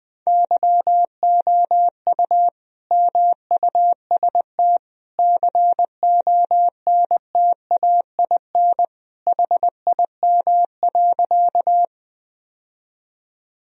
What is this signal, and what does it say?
Morse code, heard as a keyed tone, about 20 wpm, 700 Hz. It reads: YOU MUST CONTAIN HIM.
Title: Sentences from Top 1000 Words - Review Part 2 - 20wpm